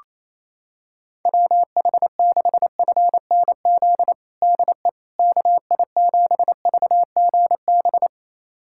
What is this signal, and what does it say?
Morse code, heard as a keyed tone, about 28 wpm, 700 Hz. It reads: WH6FNZ DE KI7VGB
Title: USA Callsign DE Callsign - 6 Characters - 28wpm